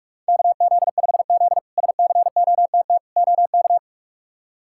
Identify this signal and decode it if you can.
Morse code, heard as a keyed tone, about 46 wpm, 700 Hz. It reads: KZ5Z SCOTT OK
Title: K1USN SST QSO Practice - 45wpm